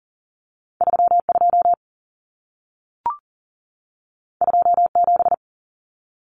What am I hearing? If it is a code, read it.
Morse code, heard as a keyed tone, about 40 wpm, 700 Hz. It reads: 32 27
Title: Common Cut Numbers - 2-Digit 40wpm